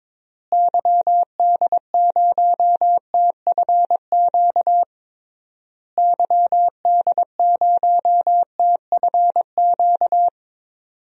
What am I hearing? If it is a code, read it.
Morse code, heard as a keyed tone, about 22 wpm, 700 Hz. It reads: YD0TFQ YD0TFQ
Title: DXCC-100 Call Signs - 6 Characters - Repeated 3x - 22wpm